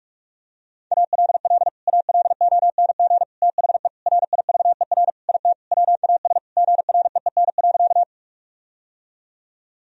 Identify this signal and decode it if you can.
Morse code, heard as a keyed tone, about 46 wpm, 700 Hz. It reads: ALL ALONG THE RIVER IT WAS GREEN.
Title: Sentences from Top 1000 Words - Review Part 2 - 45wpm